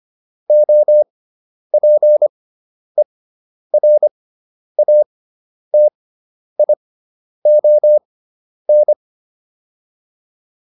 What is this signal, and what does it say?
Morse code, heard as a keyed tone, about 25 wpm, spaced as for 10 wpm, 600 Hz. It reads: OPERATION